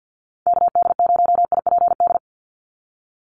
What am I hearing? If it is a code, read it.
Morse code, heard as a keyed tone, about 50 wpm, 700 Hz. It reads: KD0IPD
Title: USA Call Signs 50wpm